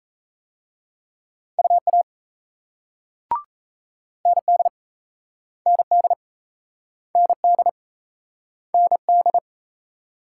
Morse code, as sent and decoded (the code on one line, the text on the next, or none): ..- .- / -. -.. / -. -.. / -. -.. / -. -..
UA ND ND ND ND